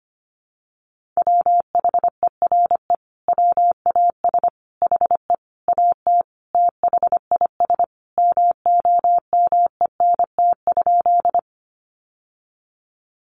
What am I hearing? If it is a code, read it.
Morse code, heard as a keyed tone, about 25 wpm, 700 Hz. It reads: WHERE WAS HE AT THIS MOMENT?
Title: Sentences from Top 1000 Words 25wpm